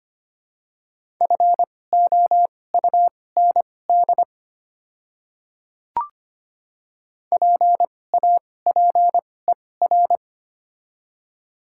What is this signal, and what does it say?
Morse code, heard as a keyed tone, about 25 wpm, spaced as for 18 wpm, 700 Hz. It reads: FOUND PAPER
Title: 2x Character Spacing - Top 500 Words - 25wpm